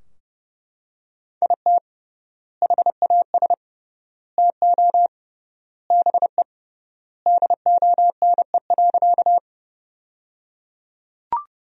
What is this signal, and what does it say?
Morse code, heard as a keyed tone, about 30 wpm, 700 Hz. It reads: IT HAS TO BE DONE.
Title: 3x Word-Spacing - Sentences from Top 500 Words - 30wpm